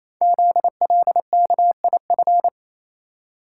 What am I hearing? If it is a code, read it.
Morse code, heard as a keyed tone, about 27 wpm, 700 Hz. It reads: ZLKIF
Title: Code Groups - 5 Letters - 28wpm